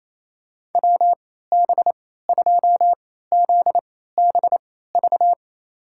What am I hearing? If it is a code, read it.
Morse code, heard as a keyed tone, about 28 wpm, spaced as for 16 wpm, 700 Hz. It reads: WB2ZBV
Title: USA Call Signs - 6 Characters - 3x Character Spacing - 28wpm